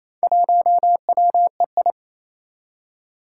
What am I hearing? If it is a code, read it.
Morse code, heard as a keyed tone, about 28 wpm, 700 Hz. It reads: 1WEI
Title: Four Challenging Letter-Number Combinations 28wpm